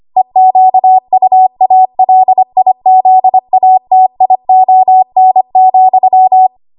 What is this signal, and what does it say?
Morse code, heard as a keyed tone, about 25 wpm, 750 Hz. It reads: EQUALIZATION,
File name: Morse / 25WPM